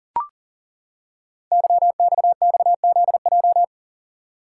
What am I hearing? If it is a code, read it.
Morse code, heard as a keyed tone, about 42 wpm, 690 Hz. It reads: YXXZJ
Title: Code Groups - 5 Letters - 40wpm